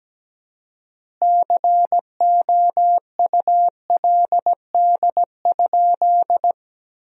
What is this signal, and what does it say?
Morse code, heard as a keyed tone, about 17 wpm, 700 Hz. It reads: COULD?
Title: 4x Word-Spacing - Sentences from Top 400 Words - 17wpm